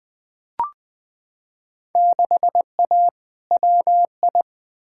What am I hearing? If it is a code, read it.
Morse code, heard as a keyed tone, about 20 wpm, 700 Hz. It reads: 6A WI
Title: Field Day Exchanges - 20wpm